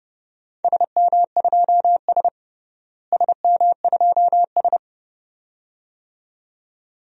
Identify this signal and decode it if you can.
Morse code, heard as a keyed tone, about 30 wpm, 700 Hz. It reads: SM2S SM2S
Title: DXCC-100 Call Signs - 4 Characters - Repeated 2x - 30wpm